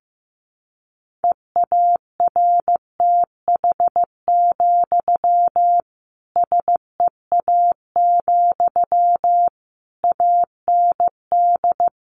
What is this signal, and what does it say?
Morse code, heard as a keyed tone, about 15 wpm, 700 Hz. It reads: EARTH, SEA, AND
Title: Sentences from Top 1500 Words 15wpm